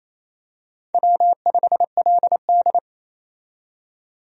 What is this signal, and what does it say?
Morse code, heard as a keyed tone, about 28 wpm, 700 Hz. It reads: W5LD
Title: USA Call Signs - 4 Characters - 28wpm